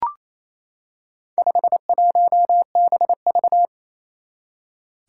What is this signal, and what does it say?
Morse code, heard as a keyed tone, about 28 wpm, 700 Hz. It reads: 51BV
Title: Four Challenging Letter-Number Combinations 28wpm